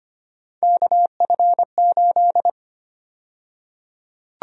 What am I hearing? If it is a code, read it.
Morse code, heard as a keyed tone, about 25 wpm, 700 Hz. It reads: KF8
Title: USA Call Sign Prefixes - 25wpm